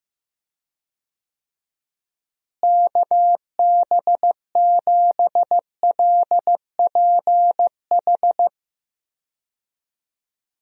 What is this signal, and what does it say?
Morse code, heard as a keyed tone, about 15 wpm, 700 Hz. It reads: KB7LPH
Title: USA Call Signs - 6 Characters - Repeated 3x - 15wpm